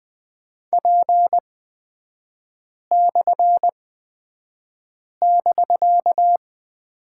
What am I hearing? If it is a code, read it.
Morse code, heard as a keyed tone, about 20 wpm, spaced as for 5 wpm, 700 Hz. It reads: P/<BK>
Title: 250 Lesson 40e - BK - Three Characters 20wpm